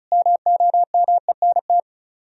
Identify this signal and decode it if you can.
Morse code, heard as a keyed tone, about 43 wpm, 700 Hz. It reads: MOMENT
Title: Speed-Racing - Sets of 5 Words 35wpm